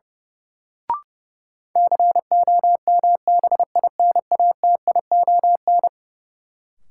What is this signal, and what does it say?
Morse code, heard as a keyed tone, about 30 wpm, 700 Hz. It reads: COMBINATION
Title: Sentences from Top 1300 Words - New Word Drill Alt - 30wpm